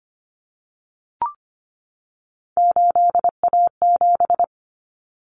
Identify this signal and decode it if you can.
Morse code, heard as a keyed tone, about 25 wpm, 700 Hz. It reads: 8A7
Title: Common Cut Numbers - 3-Digit 25wpm